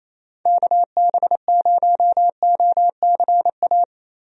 Morse code, heard as a keyed tone, about 28 wpm, 700 Hz. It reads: KB0OCA
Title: USA Call Signs - 6 Characters - Repeated 2x - 28wpm